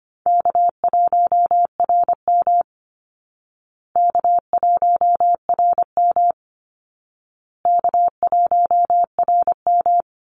Morse code, heard as a keyed tone, about 25 wpm, 700 Hz. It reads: K1RM K1RM K1RM
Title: Easy Call Signs Repeated 3x - 25wpm